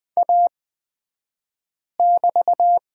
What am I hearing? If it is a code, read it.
Morse code, heard as a keyed tone, about 20 wpm, spaced as for 5 wpm, 700 Hz. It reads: A<BT>
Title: 242 Lesson 39d - Letter z - Two Characters 20wpm